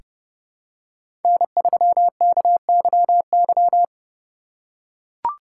Morse code, heard as a keyed tone, about 31 wpm, 690 Hz. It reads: N3KYY
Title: USA Call Signs - 4x Character Spacing - 30wpm